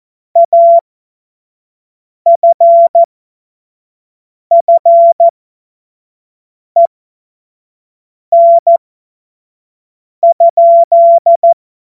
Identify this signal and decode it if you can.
Morse code, heard as a keyed tone, about 14 wpm, spaced as for 5 wpm, 680 Hz. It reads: AFFEN?
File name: Morsekurs / Witze / Witz 25. Oktober 2022 25 F70